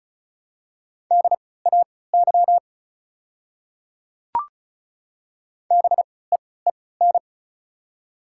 Morse code, heard as a keyed tone, about 34 wpm, spaced as for 20 wpm, 700 Hz. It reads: DAY BEEN
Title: 3x Character Spacing - Top 200 Words - 35wpm